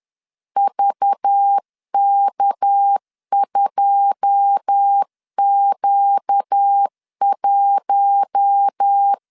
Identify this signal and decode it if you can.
Morse code, heard as a keyed tone, about 10 wpm, 800 Hz. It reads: VK2Q1
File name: LEZIONI / 10 SIGNS / 10D COMA 10wpm 3013